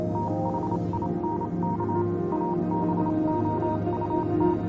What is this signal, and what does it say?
Morse code, heard as a keyed tone, about 27 wpm, 940 Hz, with noise. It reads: THINK ABOUT
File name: Vylet Pony / aura / do you remember the song she sang (morse code version)